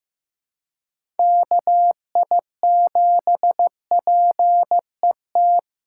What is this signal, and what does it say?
Morse code, heard as a keyed tone, about 16 wpm, 700 Hz. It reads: KI7PET
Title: USA Callsign DE Callsign - 6 Characters - 15wpm